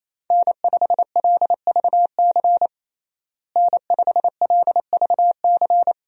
Morse code, heard as a keyed tone, about 28 wpm, 700 Hz. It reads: N5LVC N5LVC
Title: USA Call Signs - 5 Characters - Repeated 2x - 28wpm